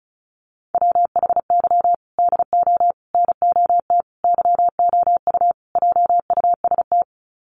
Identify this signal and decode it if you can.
Morse code, heard as a keyed tone, about 34 wpm, 700 Hz. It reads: WHY DO NOT YOU JUST
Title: Sets of 5 Words 35wpm